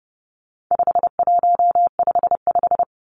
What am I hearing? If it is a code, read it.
Morse code, heard as a keyed tone, about 30 wpm, 700 Hz. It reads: 5155
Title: Numbers - 4-Digit 30wpm